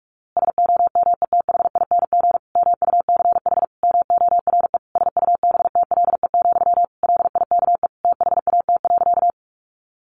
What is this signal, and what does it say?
Morse code, heard as a keyed tone, about 44 wpm, 700 Hz. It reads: SOMETHING MUCH MORE SUBTLE, LIKE THAT.
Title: A Sound Of Thunder 45wpm